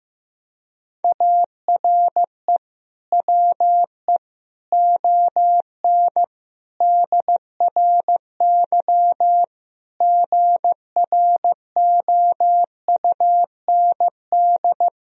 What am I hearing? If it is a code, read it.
Morse code, heard as a keyed tone, about 15 wpm, 700 Hz. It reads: ARE WE ON DRY GROUND
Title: Sentences from Top 1000 Words - Review Part 3 - 15wpm